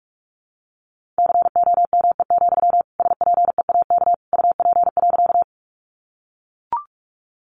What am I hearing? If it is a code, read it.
Morse code, heard as a keyed tone, about 44 wpm, 700 Hz. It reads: COME, SPEAK UP.
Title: Sentences from Top 1100 Words 45wpm